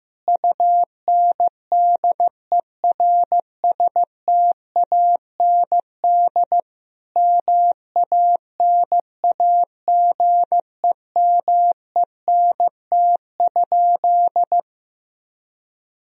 Morse code, heard as a keyed tone, about 15 wpm, 700 Hz. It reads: UNDERSTAND MANAGEMENT?
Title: Sentences from Top 1000 Words - Review Part 4 - 15wpm